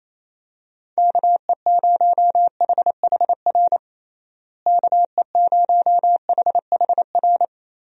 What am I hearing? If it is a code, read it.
Morse code, heard as a keyed tone, about 27 wpm, 700 Hz. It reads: KE0HHR KE0HHR
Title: USA Call Signs - 6 Characters - Repeated 3x - 28wpm